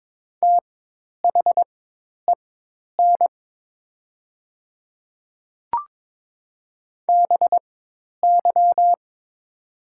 Morse code, heard as a keyed tone, about 22 wpm, spaced as for 10 wpm, 700 Hz. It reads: THEN BY
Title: 4x Character Spacing - Top 200 Words - 22wpm